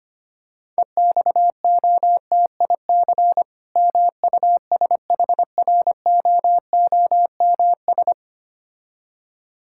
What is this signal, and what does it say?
Morse code, heard as a keyed tone, about 25 wpm, 690 Hz. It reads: EXOTIC MUSHROOMS
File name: practice / 25wpm / Difficult Words with Hints - 25wpm